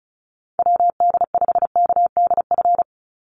Morse code, heard as a keyed tone, about 34 wpm, 700 Hz. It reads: WD5KDF
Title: USA Call Signs 35wpm